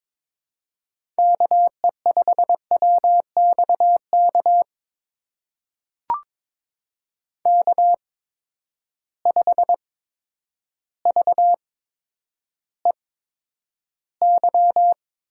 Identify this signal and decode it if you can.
Morse code, heard as a keyed tone, about 22 wpm, spaced as for 10 wpm, 700 Hz. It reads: KE5WXK K5VEY
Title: USA Call Signs - 8x Character Spacing - 22wpm